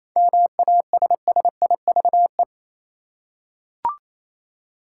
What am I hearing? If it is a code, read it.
Morse code, heard as a keyed tone, about 28 wpm, 700 Hz. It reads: MASSIVE
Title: Seven Letter Words - 28wpm